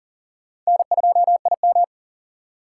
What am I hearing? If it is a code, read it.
Morse code, heard as a keyed tone, about 40 wpm, 700 Hz. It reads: N2IM
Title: USA Call Signs - 4 Characters - 6x Character Spacing - 40wpm